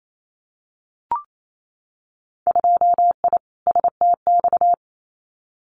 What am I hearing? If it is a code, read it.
Morse code, heard as a keyed tone, about 28 wpm, 700 Hz. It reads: 2I STX
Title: Winter Field Day Exchanges 28wpm